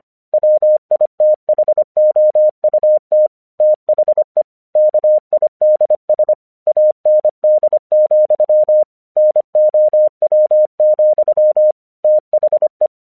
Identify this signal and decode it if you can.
Morse code, heard as a keyed tone, about 24 wpm, 610 Hz. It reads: WITHOUT THE KIDS AND, NOW, THE
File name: 1 English 25wpm / AQC05